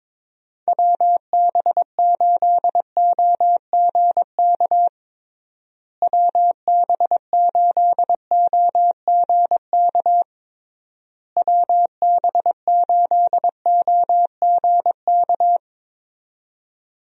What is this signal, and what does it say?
Morse code, heard as a keyed tone, about 22 wpm, 700 Hz. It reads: WB8OGK WB8OGK WB8OGK
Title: USA Call Signs - 6 Characters - Repeated 3x - 22wpm